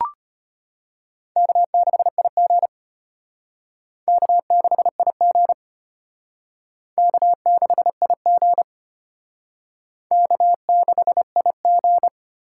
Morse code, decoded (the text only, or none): K6IG K6IG K6IG K6IG